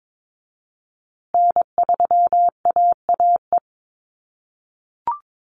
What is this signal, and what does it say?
Morse code, heard as a keyed tone, about 22 wpm, 700 Hz. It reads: N3AAE
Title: USA Call Signs Repeated 3x 22wpm